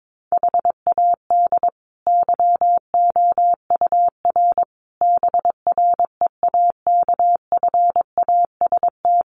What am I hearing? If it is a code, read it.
Morse code, heard as a keyed tone, about 22 wpm, 700 Hz. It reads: HAD YOUR BREAKFAST
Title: Sentences from Top 1500 Words 22wpm